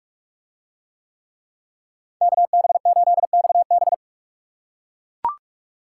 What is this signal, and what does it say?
Morse code, heard as a keyed tone, about 44 wpm, 700 Hz. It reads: KB8XB